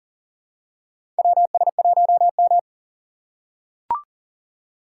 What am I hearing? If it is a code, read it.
Morse code, heard as a keyed tone, about 40 wpm, 700 Hz. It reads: WS1M
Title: USA Call Signs - 4 Characters - 2x Character Spacing - 40wpm